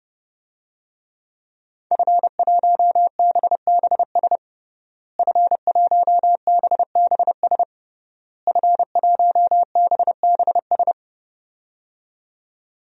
Morse code, heard as a keyed tone, about 30 wpm, 700 Hz. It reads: F1BBS F1BBS F1BBS
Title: DXCC-100 Call Signs - 5 Characters - Repeated 3x - 30wpm